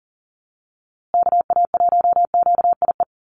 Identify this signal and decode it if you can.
Morse code, heard as a keyed tone, about 40 wpm, 700 Hz. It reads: KA1QIE
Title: USA Call Signs Repeated 3x 40wpm